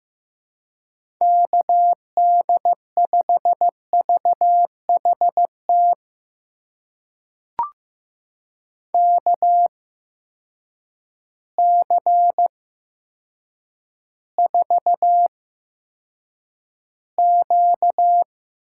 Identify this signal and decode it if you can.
Morse code, heard as a keyed tone, about 15 wpm, spaced as for 8 wpm, 700 Hz. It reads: KD5VHT KC4Q
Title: USA Call Signs - 6 Characters - 8x Character Spacing - 15wpm